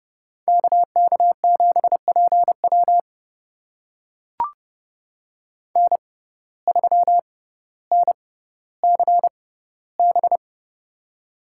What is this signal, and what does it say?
Morse code, heard as a keyed tone, about 30 wpm, spaced as for 15 wpm, 700 Hz. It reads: KK7PW N3NCB
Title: USA Call Signs - 5 Characters - 6x Character Spacing - 30wpm